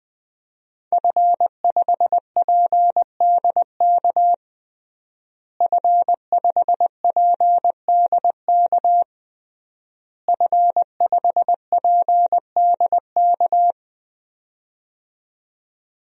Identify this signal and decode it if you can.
Morse code, heard as a keyed tone, about 21 wpm, 700 Hz. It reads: F5PDK F5PDK F5PDK
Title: DXCC-100 Call Signs - 5 Characters - Repeated 3x - 20wpm